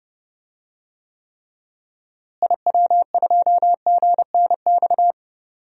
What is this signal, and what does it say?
Morse code, heard as a keyed tone, about 30 wpm, 700 Hz. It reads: IW2GNX
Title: DXCC-100 Call Signs - 6 Characters - Repeated 3x - 30wpm